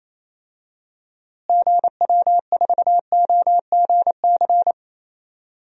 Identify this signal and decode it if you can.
Morse code, heard as a keyed tone, about 28 wpm, 700 Hz. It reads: GW4OGC
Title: DXCC-100 Call Signs - 6 Characters - Repeated 3x - 28wpm